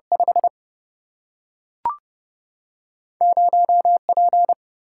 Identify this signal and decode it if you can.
Morse code, heard as a keyed tone, about 27 wpm, 700 Hz. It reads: 5 0P